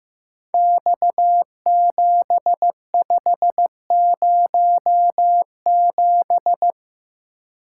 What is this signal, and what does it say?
Morse code, heard as a keyed tone, about 15 wpm, 700 Hz. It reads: X7507
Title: Code Groups - 5 Letter:Numbers - 15wpm